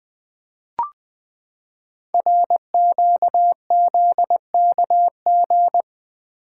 Morse code, heard as a keyed tone, about 20 wpm, 700 Hz. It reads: RQZKG
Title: Code Groups - 5 Letter:Numbers - 20wpm